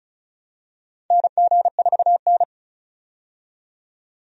Morse code, heard as a keyed tone, about 34 wpm, 700 Hz. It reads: NG4N
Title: USA Call Signs - 4 Characters - 35wpm